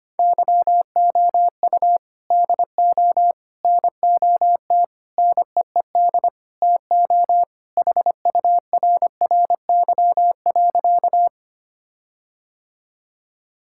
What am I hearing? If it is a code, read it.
Morse code, heard as a keyed tone, about 25 wpm, 700 Hz. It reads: YOU DO NOT NEED TO HURRY.
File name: practice / 25wpm / Sentences with 4-Word Phrases - 25wpm